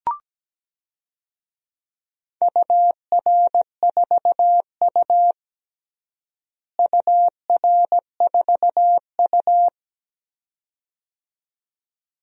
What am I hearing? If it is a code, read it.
Morse code, heard as a keyed tone, about 17 wpm, 700 Hz. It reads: UR4U UR4U
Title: DXCC-100 Call Signs - 4 Characters - Repeated 2x - 17wpm